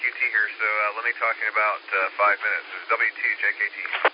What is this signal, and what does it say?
Emergency Alert System (SAME) tones, heard as harsh buzzing bursts, with speech. It is heard from 1.2 to 3.9 s.